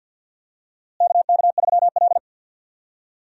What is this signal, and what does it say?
Morse code, heard as a keyed tone, about 51 wpm, 700 Hz. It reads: KK3L
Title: USA Call Signs - 4 Characters - Repeated 2x - 50wpm